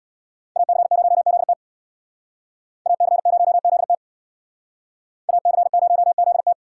Morse code, heard as a keyed tone, about 62 wpm, 700 Hz. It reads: AC0ZT AC0ZT AC0ZT
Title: Speed Racing - USA 2x2 Call Signs 50wpm